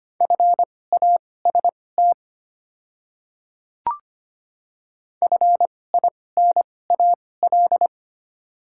Morse code, heard as a keyed tone, about 25 wpm, spaced as for 18 wpm, 700 Hz. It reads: FAST FINAL
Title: 2x Character Spacing - Top 500 Words - 25wpm